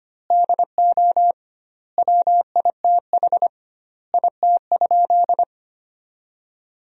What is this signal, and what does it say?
Morse code, heard as a keyed tone, about 25 wpm, 700 Hz. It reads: DO WITH IT?
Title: 2x Word-Spacing - Sentences from Top 100 Words - 25wpm